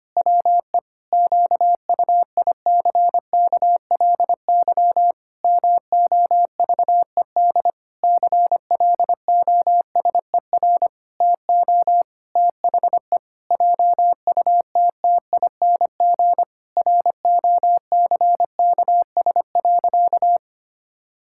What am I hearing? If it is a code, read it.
Morse code, heard as a keyed tone, about 25 wpm, 700 Hz. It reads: WE QUICKLY MOVED CLOSER TO THE JUTTING ROCKS.
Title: Pangram Challenge v2 - 25wpm